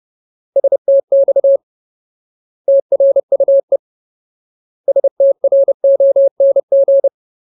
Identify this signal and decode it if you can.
Morse code, heard as a keyed tone, about 30 wpm, 550 Hz. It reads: STX TRUE STRONG
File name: Ultimate Word List / 30WPM Ultimate